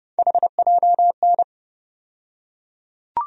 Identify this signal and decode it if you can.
Morse code, heard as a keyed tone, about 31 wpm, 700 Hz. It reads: HJN
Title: USA Call Sign Postfixes - 3 Letters 30wpm